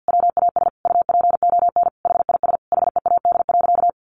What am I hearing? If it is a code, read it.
Morse code, heard as a keyed tone, about 50 wpm, 700 Hz. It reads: WAS UPON HIS HEAD.
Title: Sentences from Top 1100 Words 50wpm